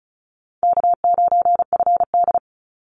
Morse code, heard as a keyed tone, about 34 wpm, 700 Hz. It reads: K9FD
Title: Call Signs Repeated 2x - 35wpm